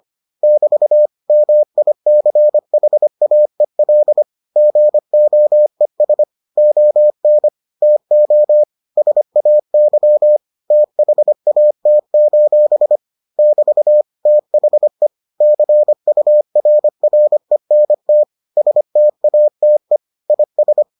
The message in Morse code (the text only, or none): <BT> MICHAEL GOES ON TO SAY THAT: <BT> THE CURRENT STATE IS